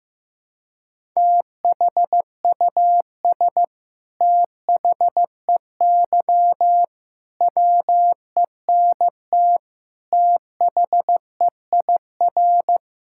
Morse code, heard as a keyed tone, about 15 wpm, 700 Hz. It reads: THUS THEY WENT THEIR
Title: Sentences from Top 1000 Words - Review Part 5 - 15wpm